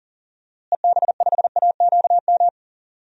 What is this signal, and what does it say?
Morse code, heard as a keyed tone, about 39 wpm, 700 Hz. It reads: EB5AQM
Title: DXCC-100 Call Signs - 6 Characters - Repeated 3x - 40wpm